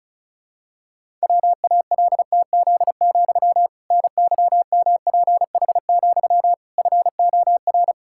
Morse code, heard as a keyed tone, about 36 wpm, 700 Hz. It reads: WALTZ, NYMPH, FOR
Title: Pangram Challenge v2 - 35wpm